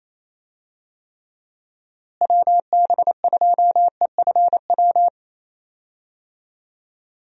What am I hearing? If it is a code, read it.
Morse code, heard as a keyed tone, about 28 wpm, 700 Hz. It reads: WB2EFW
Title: USA Call Signs - 6 Characters - Repeated 3x - 28wpm